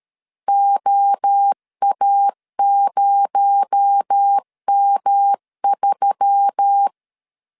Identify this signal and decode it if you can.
Morse code, heard as a keyed tone, about 13 wpm, 800 Hz. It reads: OA0M3
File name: LEZIONI / 6 PKX0 / 6 PKX0 12wpm 2231